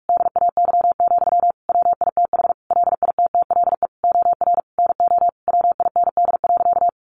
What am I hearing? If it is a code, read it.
Morse code, heard as a keyed tone, about 44 wpm, 700 Hz. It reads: DAY, WITH LITTLE OR NO WIND.
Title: Sentences from Top 1400 Words 45wpm